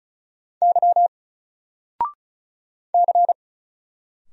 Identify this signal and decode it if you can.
Morse code, heard as a keyed tone, about 35 wpm, 700 Hz. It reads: Y C
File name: Single-Letter-Number-35wpm